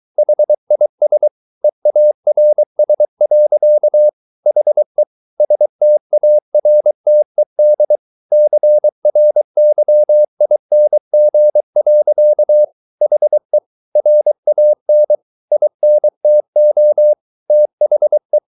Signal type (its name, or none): Morse code